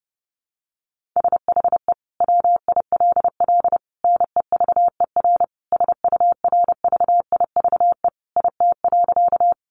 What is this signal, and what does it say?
Morse code, heard as a keyed tone, about 30 wpm, 700 Hz. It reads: SHE WILL NEVER SURVIVE IT.
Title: Sentences from Top 1300 Words 30wpm